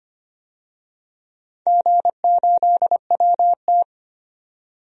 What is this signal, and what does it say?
Morse code, heard as a keyed tone, about 25 wpm, 700 Hz. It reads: G8WT
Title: DXCC-100 Call Signs - 4 Characters - 25wpm